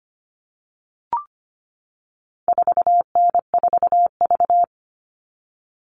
Morse code, heard as a keyed tone, about 25 wpm, 700 Hz. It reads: 4N4V
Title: All Cut Numbers - 4-Digit 25wpm